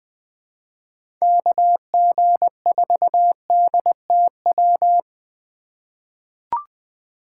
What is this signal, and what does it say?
Morse code, heard as a keyed tone, about 20 wpm, 700 Hz. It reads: KG4DTW
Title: USA Call Signs - 6 Characters - 2x Character Spacing - 20wpm